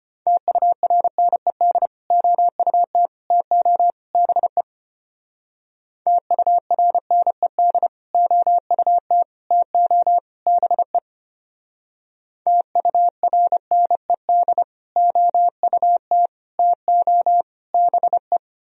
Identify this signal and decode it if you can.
Morse code, heard as a keyed tone, about 30 wpm, 700 Hz. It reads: TURNED OUT TO BE TURNED OUT TO BE TURNED OUT TO BE
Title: Speed-Racing - Sets of 4 Words 25wpm